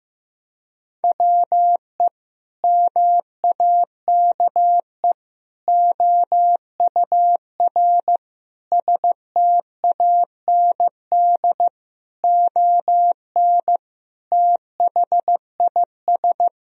Morse code, heard as a keyed tone, about 15 wpm, 700 Hz. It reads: WE MAKE OUR STAND ON THIS
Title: Sentences from Top 1000 Words - Review Part 2 - 15wpm